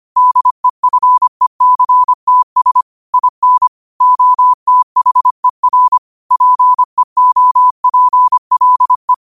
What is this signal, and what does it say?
Morse code, heard as a keyed tone, about 25 wpm, 1000 Hz. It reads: DEFECTS IN OTHER PEOPLE
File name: qotdfiles / 25wpm